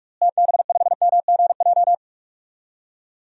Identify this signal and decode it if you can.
Morse code, heard as a keyed tone, about 46 wpm, 700 Hz. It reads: TB5MGJ
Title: DXCC-100 Call Signs - 6 Characters - Repeated 3x - 45wpm